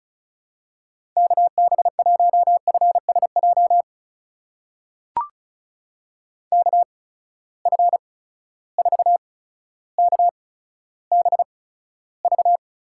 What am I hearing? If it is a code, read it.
Morse code, heard as a keyed tone, about 36 wpm, spaced as for 15 wpm, 700 Hz. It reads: KB1FSJ KF4KBV
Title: USA Call Signs - 6 Characters - 8x Character Spacing - 35wpm